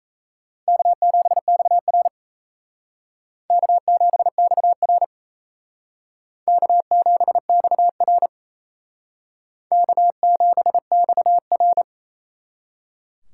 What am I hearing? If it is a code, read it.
Morse code, heard as a keyed tone, about 37 wpm, 700 Hz. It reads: K7XR K7XR K7XR K7XR